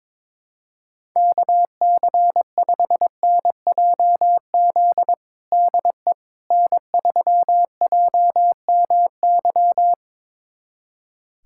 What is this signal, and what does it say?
Morse code, heard as a keyed tone, about 22 wpm, 700 Hz. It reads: KC5NJZ DE N3JMY
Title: USA Callsign DE Callsign 22wpm